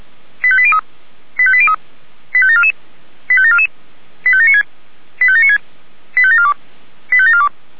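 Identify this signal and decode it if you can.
Five-tone selective call (ZVEI), heard as quick runs of five tones, tones of about 70 ms. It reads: 85803 85803 85744 85744 85796 85796 85742 85742
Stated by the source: ZVEI five-tone call received by SDR.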